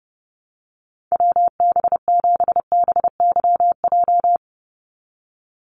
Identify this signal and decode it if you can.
Morse code, heard as a keyed tone, about 30 wpm, 700 Hz. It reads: WB7BYJ
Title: USA Call Signs 30wpm